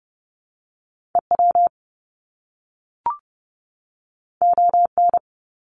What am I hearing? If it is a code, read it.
Morse code, heard as a keyed tone, about 30 wpm, 700 Hz. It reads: EW ON
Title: Top Two Letter Combinations in Words 30wpm